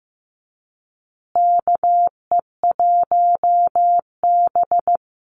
Morse code, heard as a keyed tone, about 15 wpm, 700 Hz. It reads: KE1B
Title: Call Signs Repeated 2x - 15wpm